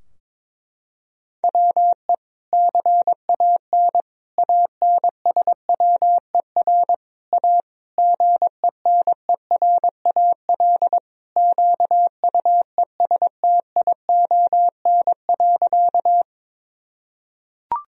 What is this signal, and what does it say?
Morse code, heard as a keyed tone, about 21 wpm, 700 Hz. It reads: WE CAN ANSWER A GENERAL QUESTION.
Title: Sentences from Top 1000 Words - Review Part 3 - 22wpm